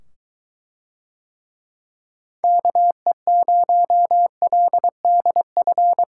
Morse code, heard as a keyed tone, about 22 wpm, 700 Hz. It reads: KE0LDF